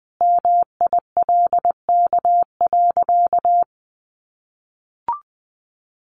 Morse code, heard as a keyed tone, about 20 wpm, 700 Hz. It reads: MILK.
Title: Easy Phrases 20wpm